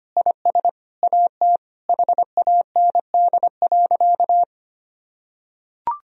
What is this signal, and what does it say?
Morse code, heard as a keyed tone, about 25 wpm, 700 Hz. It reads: IS AT HAND.
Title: Sentences from Top 1000 Words - Review Part 4 - 25wpm